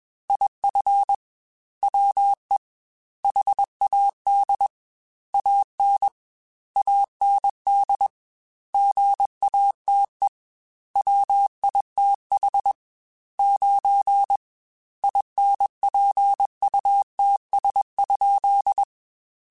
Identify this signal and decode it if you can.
Morse code, heard as a keyed tone, about 21 wpm, 800 Hz. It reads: IF WE HAD AN AND GATE WITH 9 INPUTS?
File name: Technical Subjects / Technical Subjects-mp3s / AND Gate @20